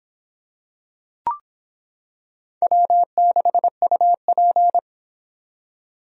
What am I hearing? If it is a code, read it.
Morse code, heard as a keyed tone, about 26 wpm, 700 Hz. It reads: W6UP